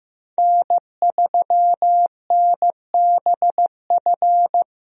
Morse code, heard as a keyed tone, about 15 wpm, 700 Hz. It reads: N3NBF
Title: USA Call Signs - 5 Characters - Repeated 2x - 15wpm